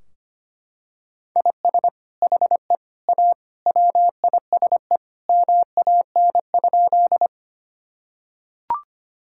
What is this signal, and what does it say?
Morse code, heard as a keyed tone, about 25 wpm, 700 Hz. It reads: IS HE A WISE MAN?
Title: Sentences from Top 1600 Words 25wpm